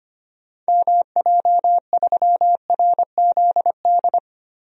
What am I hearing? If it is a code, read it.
Morse code, heard as a keyed tone, about 25 wpm, 700 Hz. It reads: MJ3RZD
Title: DXCC-100 Call Signs - 6 Characters - Repeated 3x - 25wpm